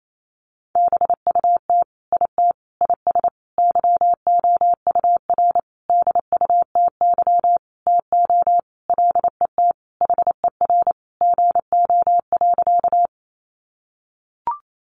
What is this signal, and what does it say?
Morse code, heard as a keyed tone, about 28 wpm, 700 Hz. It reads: BUT IT IS YOUR DUTY TO LET HER GO.
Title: Sentences from Top 1500 Words 28wpm